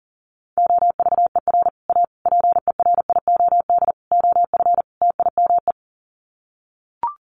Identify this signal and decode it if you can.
Morse code, heard as a keyed tone, about 40 wpm, 700 Hz. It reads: OVER A PERIOD OF TIME
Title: Sets of 5 Words Encore2 40wpm